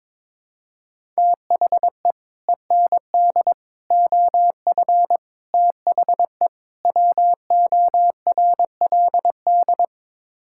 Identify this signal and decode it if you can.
Morse code, heard as a keyed tone, about 21 wpm, 700 Hz. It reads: THE END OF THE WORLD
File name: practice / 22wpm / Build A Phrase - 5-Word Phrases - 22wpm